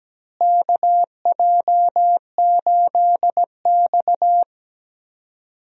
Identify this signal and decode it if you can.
Morse code, heard as a keyed tone, about 17 wpm, 700 Hz. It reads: KJ8X
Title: USA Call Signs - 4 Characters - Repeated 3x - 17wpm